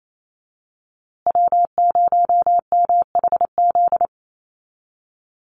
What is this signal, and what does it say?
Morse code, heard as a keyed tone, about 28 wpm, 700 Hz. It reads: W0MHZ
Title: USA Call Signs Repeated 3x 28wpm